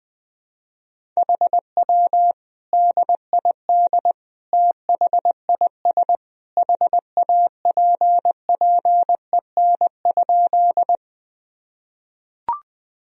Morse code, heard as a keyed tone, about 20 wpm, 700 Hz. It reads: HW DID THIS HAPPEN?
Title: Sentences from Top 400 Words and Abbreviations 20wpm